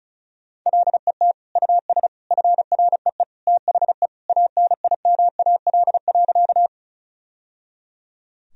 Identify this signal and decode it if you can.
Morse code, heard as a keyed tone, about 36 wpm, 700 Hz. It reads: LET US FREE THE ANIMAL.
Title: Sentences from Top 1000 Words - Review Part 1 - 35wpm